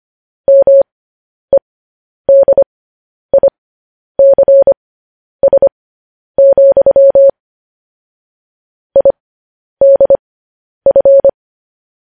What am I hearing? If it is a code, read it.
Morse code, heard as a keyed tone, about 25 wpm, spaced as for 10 wpm, 550 Hz. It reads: MEDICS, IDF
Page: News Headlines in Morse Code at 10 WPM